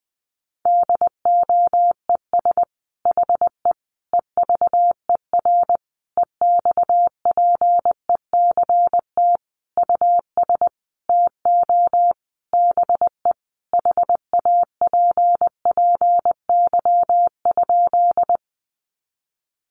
Morse code, heard as a keyed tone, about 19 wpm, 700 Hz. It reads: DOES HE EVER EXPECT US TO BE HAPPY?
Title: Sentences from Top 1100 Words 20wpm